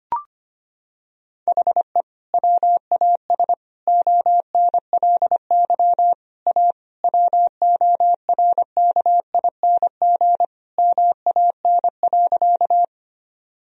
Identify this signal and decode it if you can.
Morse code, heard as a keyed tone, about 25 wpm, 700 Hz. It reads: HE WAS ONLY A WORKING MAN.
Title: Sentences from Top 1000 Words - Review Part 4 - 25wpm